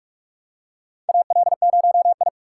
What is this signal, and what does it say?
Morse code, heard as a keyed tone, about 46 wpm, 700 Hz. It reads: AL0I
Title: USA Call Signs - 4 Characters - 3x Character Spacing - 45wpm